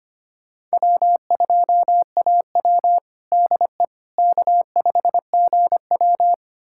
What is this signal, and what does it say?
Morse code, heard as a keyed tone, about 25 wpm, 700 Hz. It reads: W2AW DE K5GW
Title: USA Callsign DE Callsign - 4 Characters - 25wpm